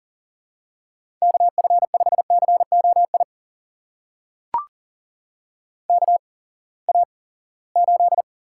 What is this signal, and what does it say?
Morse code, heard as a keyed tone, about 41 wpm, spaced as for 23 wpm, 700 Hz. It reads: KF5COI KA8
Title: USA Call Signs - 6 Characters - 8x Character Spacing - 40wpm